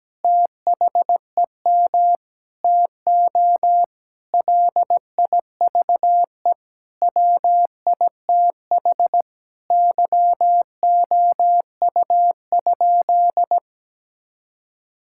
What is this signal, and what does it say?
Morse code, heard as a keyed tone, about 17 wpm, 700 Hz. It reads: THEM TO LIVE WITH YOU?
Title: Sentences from Top 1600 Words 17wpm